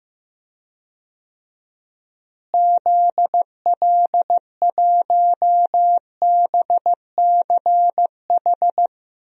Morse code, heard as a keyed tone, about 15 wpm, 700 Hz. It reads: ZL1BCH
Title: DXCC-100 Call Signs - 6 Characters - Repeated 2x - 15wpm